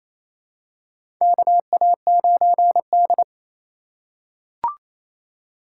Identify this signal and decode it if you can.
Morse code, heard as a keyed tone, about 27 wpm, 700 Hz. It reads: KA9D
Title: USA Call Signs - 4 Characters - 3x Character Spacing - 28wpm